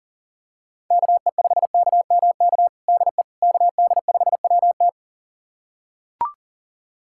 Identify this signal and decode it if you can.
Morse code, heard as a keyed tone, about 40 wpm, 700 Hz. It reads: KE5KMK DE KD5WT
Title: USA Callsign DE Callsign 40wpm